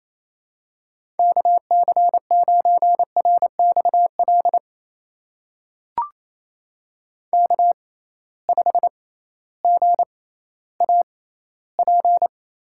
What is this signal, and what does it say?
Morse code, heard as a keyed tone, about 28 wpm, spaced as for 15 wpm, 700 Hz. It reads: KC9RXL K5GAP
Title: USA Call Signs - 6x Character Spacing - 28wpm